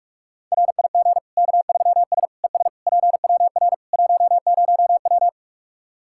Morse code, heard as a keyed tone, about 41 wpm, 700 Hz. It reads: RIG K3S ES PWR 10W
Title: Practice QSO - Exchange 2 - Info - 45wpm